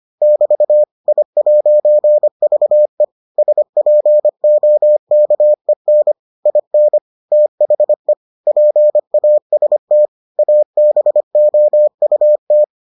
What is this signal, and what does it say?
Morse code, heard as a keyed tone, about 25 wpm, 600 Hz. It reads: <BT> I'VE SPOKEN IN THE PAST ABOUT